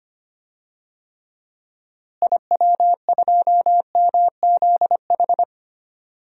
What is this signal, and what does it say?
Morse code, heard as a keyed tone, about 25 wpm, 700 Hz. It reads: IW2MZH
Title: DXCC-100 Call Signs - 6 Characters - Repeated 3x - 25wpm